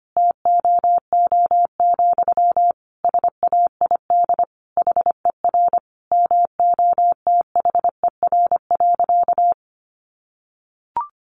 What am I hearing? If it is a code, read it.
Morse code, heard as a keyed tone, about 25 wpm, 700 Hz. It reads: TOO, SAID HER MOTHER.
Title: Sentences from Top 700 Words 25wpm